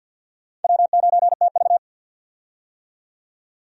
Morse code, heard as a keyed tone, about 47 wpm, 700 Hz. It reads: W9TV
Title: USA Call Signs - 4 Characters - 50wpm